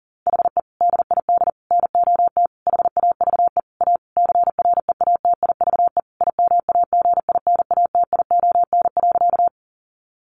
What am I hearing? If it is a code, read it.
Morse code, heard as a keyed tone, about 37 wpm, 700 Hz. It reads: HE DID NOT HAVE A CREATIVE IMAGINATION.
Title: Sentences from Top 1300 Words 40wpm